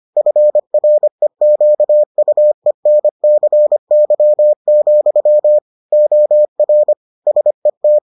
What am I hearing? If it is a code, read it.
Morse code, heard as a keyed tone, about 25 wpm, 600 Hz. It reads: FREQUENCY, OR SET